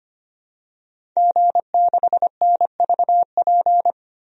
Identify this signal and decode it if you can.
Morse code, heard as a keyed tone, about 25 wpm, 700 Hz. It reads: G6NVP
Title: DXCC-100 Call Signs - 5 Characters - 25wpm